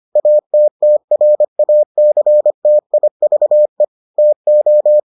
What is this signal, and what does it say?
Morse code, heard as a keyed tone, about 25 wpm, 600 Hz. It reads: ATTRACTIVE TO